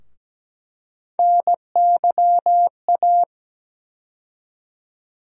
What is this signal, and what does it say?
Morse code, heard as a keyed tone, about 17 wpm, 700 Hz. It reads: NYA
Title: USA Call Sign Postfixes - 3 Letters 17wpm